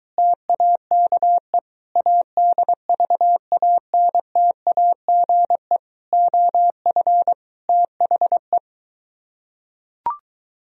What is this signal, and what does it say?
Morse code, heard as a keyed tone, about 23 wpm, 700 Hz. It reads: TAKE ADVANTAGE OF THE